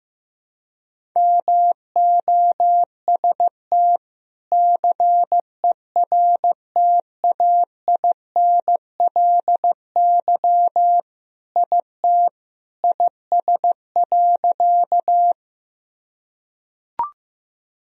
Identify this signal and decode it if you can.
Morse code, heard as a keyed tone, about 15 wpm, 700 Hz. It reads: MOST CERTAINLY IT IS.
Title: Sentences from Top 1000 Words - Review Part 1 - 15wpm